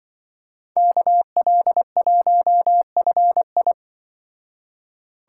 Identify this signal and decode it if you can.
Morse code, heard as a keyed tone, about 24 wpm, 700 Hz. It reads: KL1FI
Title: Speed Racing - USA 2x2 Call Signs 20wpm